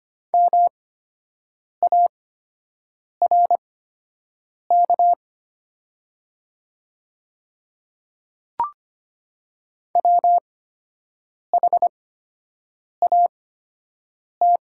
Morse code, heard as a keyed tone, about 25 wpm, spaced as for 7 wpm, 700 Hz. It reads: MARK WHAT
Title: 8x Character Spacing - Top 300 Words - 25wpm